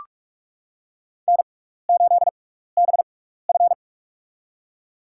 Morse code, heard as a keyed tone, about 46 wpm, spaced as for 15 wpm, 700 Hz. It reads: N8BF
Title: USA Call Signs - 4 Characters - 6x Character Spacing - 45wpm